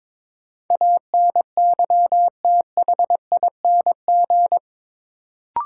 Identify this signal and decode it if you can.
Morse code, heard as a keyed tone, about 22 wpm, 700 Hz. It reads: ANYTHING
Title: Sets of 4 Words - New Word Drill - 22wpm